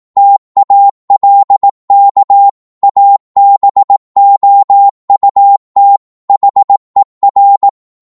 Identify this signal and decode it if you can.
Morse code, heard as a keyed tone, about 18 wpm, 800 Hz. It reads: TALK ABOUT HER
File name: Morse / psg-18wpm